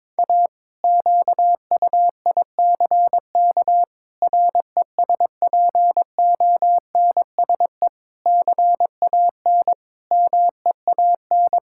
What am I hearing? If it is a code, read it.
Morse code, heard as a keyed tone, about 22 wpm, 700 Hz. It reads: A QUICK RESPONSE CAN MEAN
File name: practice / 22wpm / Sentences with 8-Word Phrases - 22wpm